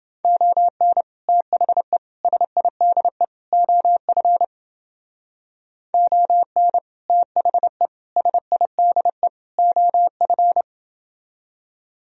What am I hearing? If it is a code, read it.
Morse code, heard as a keyed tone, about 30 wpm, 700 Hz. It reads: ON THE SIDE OF ON THE SIDE OF